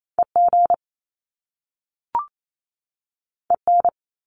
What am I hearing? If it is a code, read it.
Morse code, heard as a keyed tone, about 28 wpm, 700 Hz. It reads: EG EN
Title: Top Two Letter Combinations in Words 28wpm